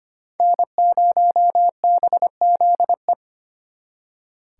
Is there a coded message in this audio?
Yes: Morse code from 0.4 to 3.1 s, heard as a keyed tone, about 33 wpm, 700 Hz. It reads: N0BZE